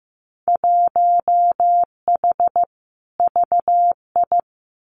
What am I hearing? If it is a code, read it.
Morse code, heard as a keyed tone, about 15 wpm, 700 Hz. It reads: 1H VI
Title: Winter Field Day Exchanges Encore 15wpm